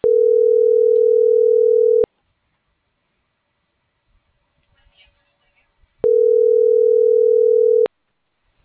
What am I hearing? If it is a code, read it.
Ringback tone (North America), 440 + 480 Hz.